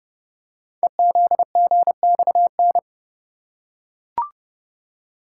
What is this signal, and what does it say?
Morse code, heard as a keyed tone, about 30 wpm, 700 Hz. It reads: EZGXN
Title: Code Groups - 5 Letter:Numbers - 30wpm